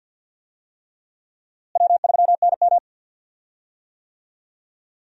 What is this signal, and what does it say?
Morse code, heard as a keyed tone, about 51 wpm, 700 Hz. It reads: W3NM
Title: USA Call Signs - 4 Characters - Repeated 3x - 50wpm